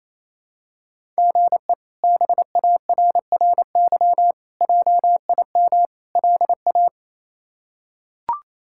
Morse code, heard as a keyed tone, about 27 wpm, 700 Hz. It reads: GE BARRY JIM LA
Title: K1USN SST QSO Practice - 28wpm